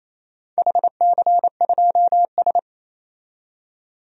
Morse code, heard as a keyed tone, about 27 wpm, 700 Hz. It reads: HC2S
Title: DXCC-100 Call Signs - 4 Characters - 28wpm